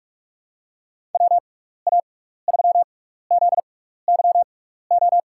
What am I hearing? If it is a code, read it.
Morse code, heard as a keyed tone, about 46 wpm, spaced as for 16 wpm, 700 Hz. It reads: WA3ZYO
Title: USA Call Signs - 6x Character Spacing - 45wpm